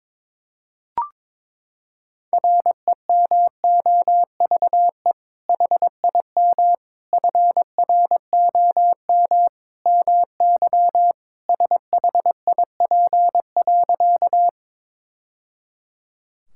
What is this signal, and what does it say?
Morse code, heard as a keyed tone, about 22 wpm, 700 Hz. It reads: REMOVE HIM FROM MY SHIP.
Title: Sentences from Top 1000 Words - Review Part 5 - 22wpm